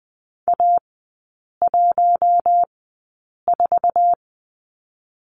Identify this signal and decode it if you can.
Morse code, heard as a keyed tone, about 19 wpm, spaced as for 8 wpm, 700 Hz. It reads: A14
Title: 127 Lesson 21e - Number 6 - Three Characters 20wpm